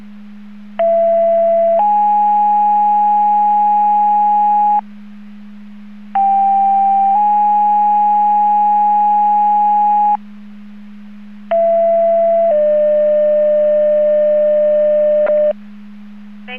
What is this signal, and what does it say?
Two-tone page, heard as a one-second tone, then a three-second tone, about 670 then 830 Hz, about 790 then 830 Hz, about 670 then 600 Hz. It is heard from 0.8 to 15.5 s.